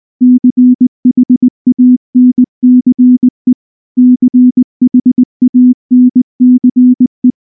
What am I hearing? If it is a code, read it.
Morse code, heard as a keyed tone, about 19 wpm, 250 Hz. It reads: CHANCE CHANCE
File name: morse / morse-training-20WPM-3